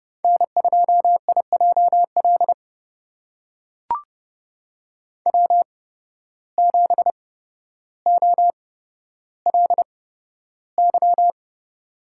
Morse code, heard as a keyed tone, about 31 wpm, spaced as for 13 wpm, 700 Hz. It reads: N2IJL W7OLY
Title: USA Call Signs - 5 Characters - 8x Character Spacing - 30wpm